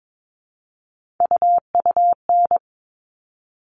Morse code, heard as a keyed tone, about 22 wpm, 700 Hz. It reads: UUN
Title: All Cut Numbers - 3-Digit 22wpm